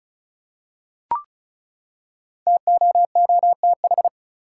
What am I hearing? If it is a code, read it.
Morse code, heard as a keyed tone, about 34 wpm, 700 Hz. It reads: TOOTH
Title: Five Letter Words - 35wpm